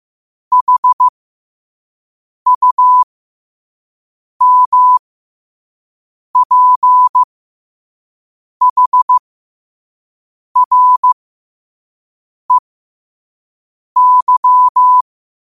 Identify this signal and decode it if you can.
Morse code, heard as a keyed tone, about 15 wpm, spaced as for 5 wpm, 1000 Hz. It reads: HUMPHREY